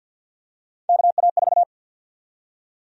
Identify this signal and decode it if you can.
Morse code, heard as a keyed tone, about 46 wpm, 700 Hz. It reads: KA4
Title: USA Call Sign Prefixes - 50wpm